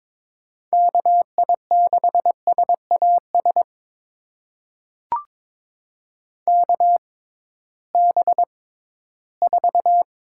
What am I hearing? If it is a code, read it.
Morse code, heard as a keyed tone, about 21 wpm, spaced as for 14 wpm, 700 Hz. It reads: KI6SAS KB4
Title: USA Call Signs - 6 Characters - 6x Character Spacing - 22wpm